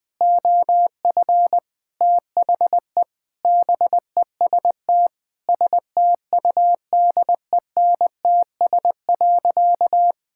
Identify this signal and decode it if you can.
Morse code, heard as a keyed tone, about 20 wpm, 700 Hz. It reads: OF THE BEST STUDENTS.
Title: Sentences with 4-Word Phrases - Spoken Beginning - 20wpm